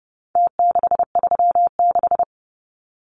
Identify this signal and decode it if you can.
Morse code, heard as a keyed tone, about 30 wpm, 700 Hz. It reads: T636
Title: Numbers - Contest Style - 4-Digit 30wpm